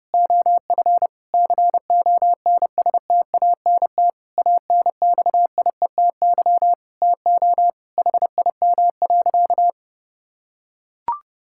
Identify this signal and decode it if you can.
Morse code, heard as a keyed tone, about 30 wpm, 700 Hz. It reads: OF CONSTANT ANXIETY TO HIM.
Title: Sentences from Top 1600 Words 30wpm